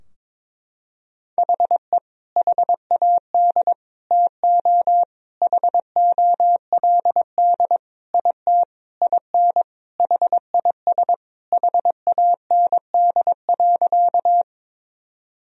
Morse code, heard as a keyed tone, about 22 wpm, 700 Hz. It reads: HE HAD TO HOLD IT IN HIS HAND.
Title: Sentences from Top 1000 Words - Review Part 2 - 22wpm